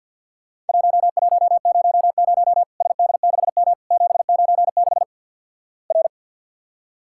Morse code, heard as a keyed tone, about 50 wpm, 700 Hz. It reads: 1100 SD6M 796 R
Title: Worked All Europe DX Contest - QTC Exchange with Repeats - 50wpm